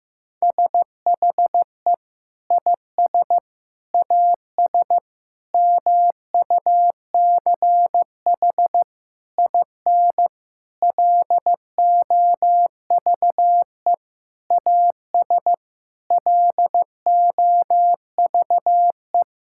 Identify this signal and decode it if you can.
Morse code, heard as a keyed tone, about 15 wpm, 700 Hz. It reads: SHE IS AS MUCH IN LOVE AS LOVE
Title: Sentences from Top 1000 Words - Review Part 5 - 15wpm